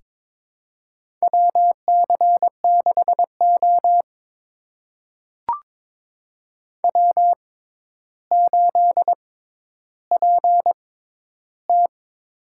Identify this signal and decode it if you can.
Morse code, heard as a keyed tone, about 22 wpm, spaced as for 11 wpm, 700 Hz. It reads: WC6O W8PT
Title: USA Call Signs - 4 Characters - 6x Character Spacing - 22wpm